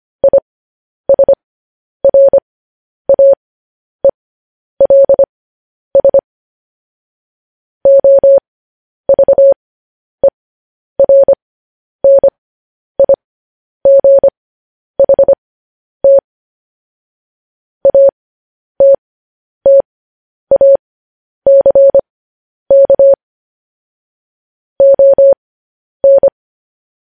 Morse code, heard as a keyed tone, about 25 wpm, spaced as for 10 wpm, 550 Hz. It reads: ISRAELS OVERNIGHT ATTACK ON